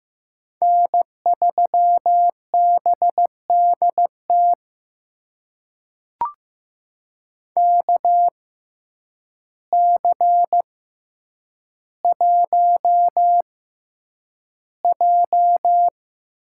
Morse code, heard as a keyed tone, about 15 wpm, spaced as for 8 wpm, 700 Hz. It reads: N3BDT KC1J
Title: USA Call Signs - 6x Character Spacing - 15wpm